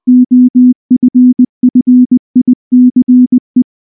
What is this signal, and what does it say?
Morse code, heard as a keyed tone, about 19 wpm, 250 Hz. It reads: OFFICE
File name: morse / morse-training-20WPM-5